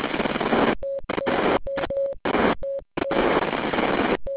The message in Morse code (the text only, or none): none